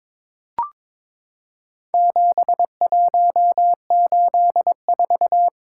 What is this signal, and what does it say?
Morse code, heard as a keyed tone, about 22 wpm, 700 Hz. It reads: 7184
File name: practice / 22wpm / Four Letter-Number Combinations - 22wpm